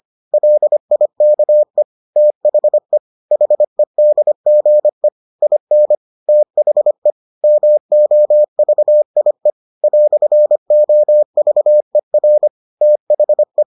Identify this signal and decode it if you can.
Morse code, heard as a keyed tone, about 25 wpm, 600 Hz. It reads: LIKE THE HEDGE IN THE MOVIE "OVER THE